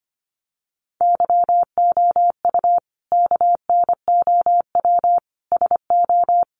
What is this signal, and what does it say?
Morse code, heard as a keyed tone, about 25 wpm, 700 Hz. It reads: YOU KNOW SO
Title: Sentences from Top 800 Words 25wpm